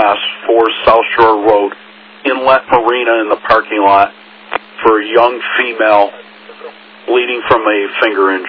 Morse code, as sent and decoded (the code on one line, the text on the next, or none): none